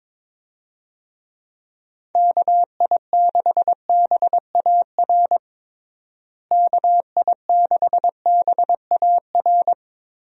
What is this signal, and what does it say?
Morse code, heard as a keyed tone, about 22 wpm, 700 Hz. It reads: KI6BAR KI6BAR
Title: USA Call Signs - 6 Characters - Repeated 2x - 22wpm